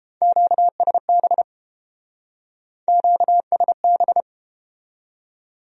Morse code, heard as a keyed tone, about 31 wpm, 700 Hz. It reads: QSB QSB